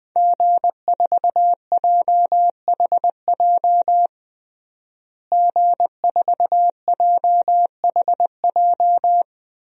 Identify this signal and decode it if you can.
Morse code, heard as a keyed tone, about 19 wpm, 700 Hz. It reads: G4JHJ G4JHJ
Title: DXCC-100 Call Signs - 5 Characters - Repeated 3x - 20wpm